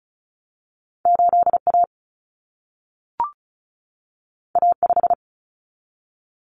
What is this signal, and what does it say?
Morse code, heard as a keyed tone, about 36 wpm, 700 Hz. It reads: 8A A5
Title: Common Cut Numbers - 2-Digit 35wpm